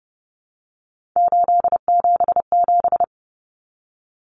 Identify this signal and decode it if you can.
Morse code, heard as a keyed tone, about 30 wpm, 700 Hz. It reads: 877
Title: Common Cut Numbers - 3-Digit 30wpm